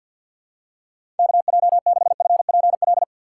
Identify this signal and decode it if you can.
Morse code, heard as a keyed tone, about 52 wpm, 700 Hz. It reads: KJ6FPL
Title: USA Call Signs - 6 Characters - 6x Character Spacing - 50wpm